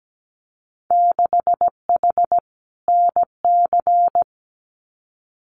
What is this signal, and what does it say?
Morse code, heard as a keyed tone, about 17 wpm, 700 Hz. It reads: 6H NC
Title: Winter Field Day Exchanges Encore 17wpm